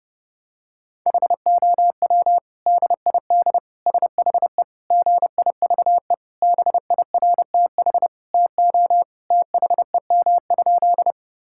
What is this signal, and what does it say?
Morse code, heard as a keyed tone, about 30 wpm, 700 Hz. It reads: HOW DID SHE GIVE BIRTH TO THEM?
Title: Sentences from Top 1000 Words - Review Part 5 - 30wpm